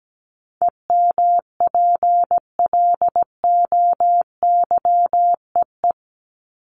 Morse code, heard as a keyed tone, about 17 wpm, 700 Hz. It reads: EMPLOYEE
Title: Top 1501-1750 Words 17wpm